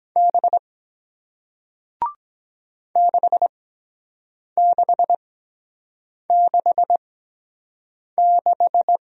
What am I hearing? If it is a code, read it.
Morse code, heard as a keyed tone, about 23 wpm, spaced as for 7 wpm, 700 Hz. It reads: B 6666